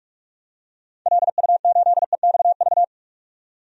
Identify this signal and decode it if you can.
Morse code, heard as a keyed tone, about 47 wpm, 700 Hz. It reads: LU8EXV
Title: DXCC-100 Call Signs - 6 Characters - 45wpm